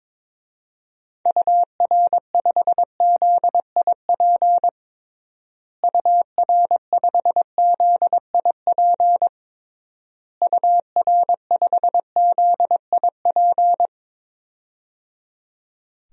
Morse code, heard as a keyed tone, about 22 wpm, 700 Hz. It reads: UR5ZIP UR5ZIP UR5ZIP
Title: DXCC-100 Call Signs - 6 Characters - Repeated 3x - 22wpm